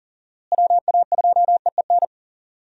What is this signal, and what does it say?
Morse code, heard as a keyed tone, about 40 wpm, 700 Hz. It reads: WA2EEN
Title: USA Call Signs - 6 Characters - Repeated 3x - 40wpm